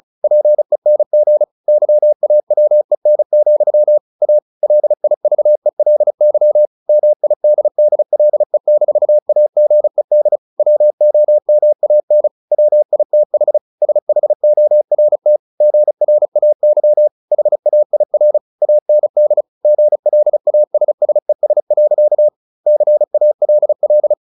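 Morse code, heard as a keyed tone, about 34 wpm, 600 Hz. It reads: PENG YAWEN, A LIVELY MIDDLE-AGED WOMAN WITH SHORT GRAY HAIR AND GLASSES. CALL
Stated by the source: Morse practice at 35 wpm from CW med Gnister, sending Ken Follett's novel Never in English.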